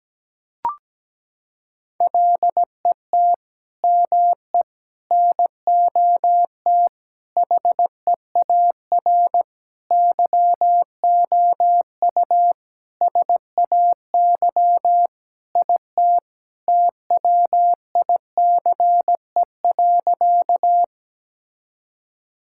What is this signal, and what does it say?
Morse code, heard as a keyed tone, about 17 wpm, 700 Hz. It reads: LET ME NOT HEAR YOU SAY IT TWICE.
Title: Sentences from Top 1600 Words 17wpm